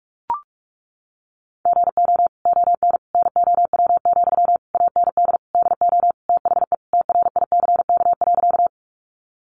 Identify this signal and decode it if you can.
Morse code, heard as a keyed tone, about 44 wpm, 700 Hz. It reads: GO ON NOW, AND DO THE TRICK.
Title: Sentences from Top 1500 Words 45wpm